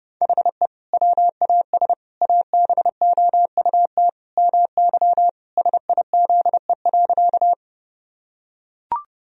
Morse code, heard as a keyed tone, about 32 wpm, 700 Hz. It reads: HE WAS ABOUT MY SIZE.
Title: Sentences from Top 1000 Words - Review Part 2 - 30wpm